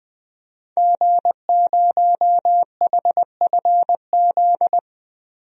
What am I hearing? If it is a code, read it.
Morse code, heard as a keyed tone, about 20 wpm, 700 Hz. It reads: G0HFZ
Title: DXCC-100 Call Signs - 5 Characters - Repeated 3x - 20wpm